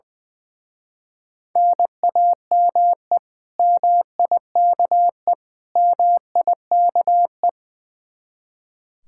Morse code, heard as a keyed tone, about 22 wpm, 700 Hz. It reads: NAME MIKE MIKE
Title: Speed-Racing Common QSO Phrases - 20wpm